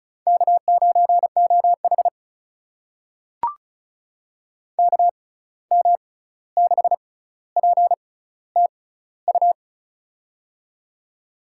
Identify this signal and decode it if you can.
Morse code, heard as a keyed tone, about 36 wpm, spaced as for 16 wpm, 700 Hz. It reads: K9OH KM6PTU
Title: USA Call Signs - 6x Character Spacing - 35wpm